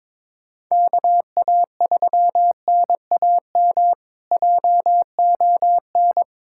Morse code, heard as a keyed tone, about 22 wpm, 700 Hz. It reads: KA3NAM JON
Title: K1USN SST QSO Practice - 22wpm